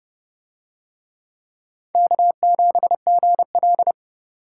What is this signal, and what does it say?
Morse code, heard as a keyed tone, about 29 wpm, 700 Hz. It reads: K7GL